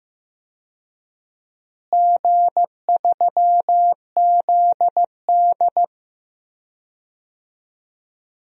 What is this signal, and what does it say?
Morse code, heard as a keyed tone, about 15 wpm, 700 Hz. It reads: G3ZD
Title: DXCC-100 Call Signs - 4 Characters - Repeated 3x - 15wpm